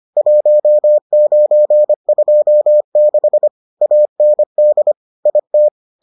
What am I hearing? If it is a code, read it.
Morse code, heard as a keyed tone, about 25 wpm, 600 Hz. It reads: 1926 AND IT